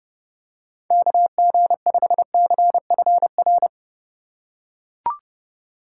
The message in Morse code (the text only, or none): KG5CFR